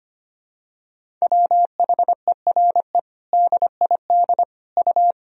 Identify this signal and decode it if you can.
Morse code, heard as a keyed tone, about 25 wpm, 700 Hz. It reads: WHERE DID U
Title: Sentences from Top 200 Words and Abbreviations 25wpm